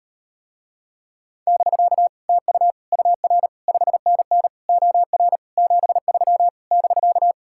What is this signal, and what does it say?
Morse code, heard as a keyed tone, about 39 wpm, 700 Hz. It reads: <BK> TU UR 5NN OR 73 <BK>